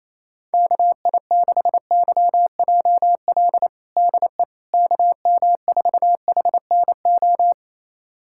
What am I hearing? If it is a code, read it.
Morse code, heard as a keyed tone, about 28 wpm, 700 Hz. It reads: KI6YJL DE KM4HNO
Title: USA Callsign DE Callsign - 6 Characters - 28wpm